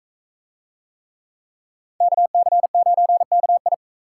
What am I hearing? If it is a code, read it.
Morse code, heard as a keyed tone, about 41 wpm, 700 Hz. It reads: KC9KI